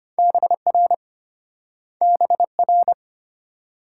Morse code, heard as a keyed tone, about 34 wpm, 700 Hz. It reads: BR BR